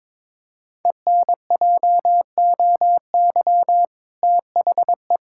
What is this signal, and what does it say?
Morse code, heard as a keyed tone, about 22 wpm, 700 Hz. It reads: ENJOY THE
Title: Sentences from Top 1000 Words - Review Part 5 - 22wpm